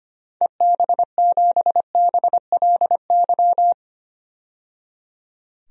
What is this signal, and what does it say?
Morse code, heard as a keyed tone, about 25 wpm, 700 Hz. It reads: EB7BLY
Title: DXCC-100 Call Signs - 6 Characters - 25wpm